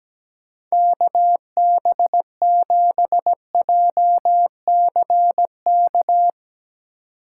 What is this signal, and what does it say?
Morse code, heard as a keyed tone, about 18 wpm, 700 Hz. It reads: KB7JCK
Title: Speed Racing - USA 2x3 Call Signs 17wpm